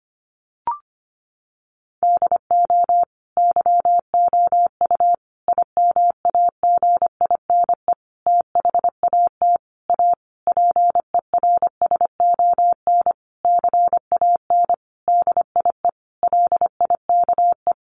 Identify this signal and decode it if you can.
Morse code, heard as a keyed tone, about 25 wpm, 700 Hz. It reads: DO YOU IMAGINE THAT A PERSON CAN DIE LIKE
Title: Sentences from Top 1400 Words 25wpm